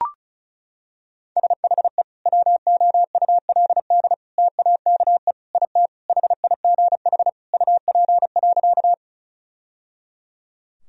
Morse code, heard as a keyed tone, about 34 wpm, 700 Hz. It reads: SHE WOULD TAKE IT HIGH UP.
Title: Sentences from Top 1000 Words - Review Part 1 - 35wpm